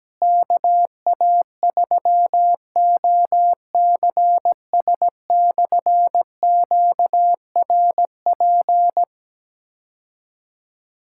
Morse code, heard as a keyed tone, about 17 wpm, 700 Hz. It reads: KA3OCS/QRP
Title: K1USN SST QSO Practice - 17wpm